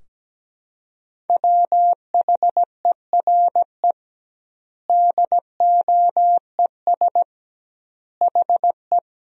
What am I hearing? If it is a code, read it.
Morse code, heard as a keyed tone, about 17 wpm, 700 Hz. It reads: WHERE DOES HE
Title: 2x Word-Spacing - Sentences from Top 200 Words - 17wpm